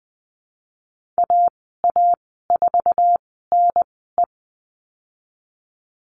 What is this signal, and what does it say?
Morse code, heard as a keyed tone, about 20 wpm, spaced as for 15 wpm, 700 Hz. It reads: AA4NE
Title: 048 Lesson 9g - Number 4 - Call Signs 20wpm